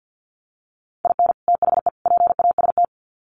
Morse code, heard as a keyed tone, about 50 wpm, 700 Hz. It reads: IN THE PAST
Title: Sets of 3 Words 50wpm